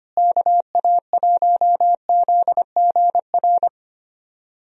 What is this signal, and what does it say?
Morse code, heard as a keyed tone, about 25 wpm, 700 Hz. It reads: KA1ZGR
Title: USA Call Signs - 6 Characters - Repeated 3x - 25wpm